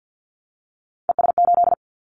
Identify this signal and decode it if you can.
Morse code, heard as a keyed tone, about 50 wpm, 700 Hz. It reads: ES8
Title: Callsign Prefixes for Top 50 DXCC - 50wpm